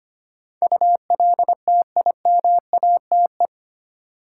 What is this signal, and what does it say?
Morse code, heard as a keyed tone, about 25 wpm, 700 Hz. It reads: ULTIMATE
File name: 8 Letter Words - First 3 Letters Given - 25wpm